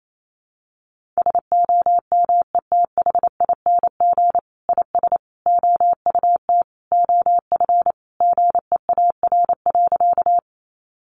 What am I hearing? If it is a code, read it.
Morse code, heard as a keyed tone, about 28 wpm, 700 Hz. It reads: SOMETHING IS OUT OF GEAR.
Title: Sentences from Top 1400 Words 28wpm